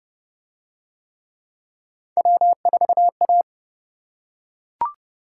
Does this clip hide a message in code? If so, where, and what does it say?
Yes: Morse code from 2.2 to 3.4 s, heard as a keyed tone, about 29 wpm, 700 Hz. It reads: W4A